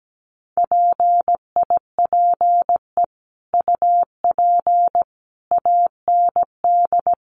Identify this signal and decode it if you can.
Morse code, heard as a keyed tone, about 17 wpm, 700 Hz. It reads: PIPE UP AND
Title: Sentences from Top 1500 Words 17wpm